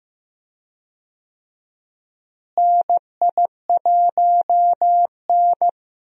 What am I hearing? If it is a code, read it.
Morse code, heard as a keyed tone, about 16 wpm, 700 Hz. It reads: NI1N
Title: USA Call Signs - 4 Characters - Repeated 3x - 15wpm